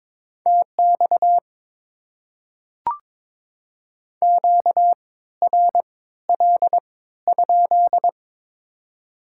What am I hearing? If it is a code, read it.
Morse code, heard as a keyed tone, about 22 wpm, spaced as for 14 wpm, 700 Hz. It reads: TX QRL?
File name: Top 90 QSO Elements - 3x Character Spacing - 22wpm